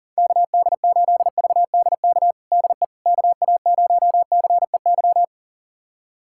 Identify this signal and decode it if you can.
Morse code, heard as a keyed tone, about 40 wpm, 700 Hz. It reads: KD8VDK DE KA0CEY
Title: USA Callsign DE Callsign 40wpm